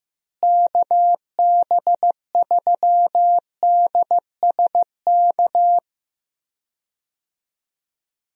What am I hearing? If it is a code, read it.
Morse code, heard as a keyed tone, about 15 wpm, 700 Hz. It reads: KB3DSK
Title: USA Call Signs - 6 Characters - Repeated 2x - 15wpm